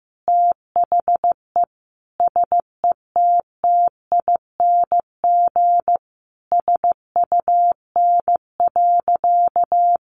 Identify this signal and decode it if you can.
Morse code, heard as a keyed tone, about 15 wpm, 700 Hz. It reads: THE SETTING SUN.
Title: Sentences from Top 1300 Words 15wpm